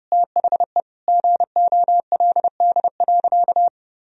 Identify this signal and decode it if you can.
Morse code, heard as a keyed tone, about 32 wpm, 700 Hz. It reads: THE GOLD.
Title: Sentences from Top 1000 Words - Review Part 3 - 30wpm